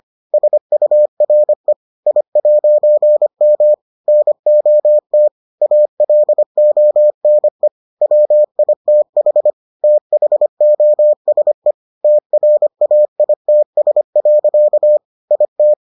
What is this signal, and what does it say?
Morse code, heard as a keyed tone, about 25 wpm, 600 Hz. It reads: SURE I'M NOT ALONE WITH THOSE TRAITS. IT